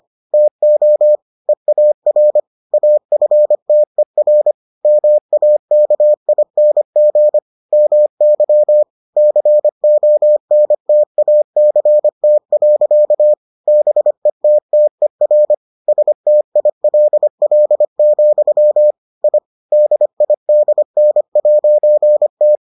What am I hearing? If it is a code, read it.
Morse code, heard as a keyed tone, about 25 wpm, 600 Hz. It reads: TO EAR AFTER MAKING MY CONTACT. BETTER STILL, I DIDN'T